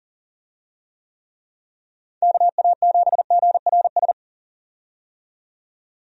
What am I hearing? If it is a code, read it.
Morse code, heard as a keyed tone, about 40 wpm, 700 Hz. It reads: KA7GRS
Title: USA Call Signs - 6 Characters - 40wpm